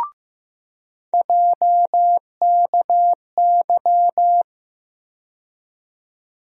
Morse code, heard as a keyed tone, about 15 wpm, 700 Hz. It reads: JKY
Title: USA Call Sign Postfixes - 3 Letters 15wpm